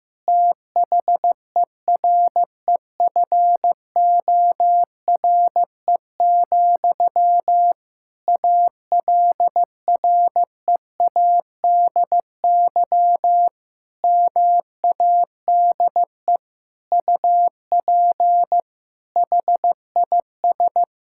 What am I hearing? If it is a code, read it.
Morse code, heard as a keyed tone, about 15 wpm, 700 Hz. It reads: THEREFORE, ALREADY MADE UP HIS
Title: Sentences from Top 1000 Words - Review Part 5 - 15wpm